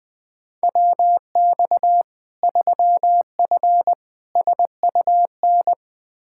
Morse code, heard as a keyed tone, about 20 wpm, 700 Hz. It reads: WX 3F SUN
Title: Practice QSO - Exchange 2 - Info - 20wpm